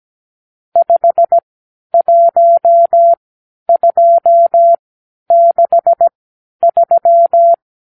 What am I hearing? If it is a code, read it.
Morse code, heard as a keyed tone, about 17 wpm, spaced as for 10 wpm, 680 Hz. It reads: 51263
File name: k7qo / Lesson 50 (1-6)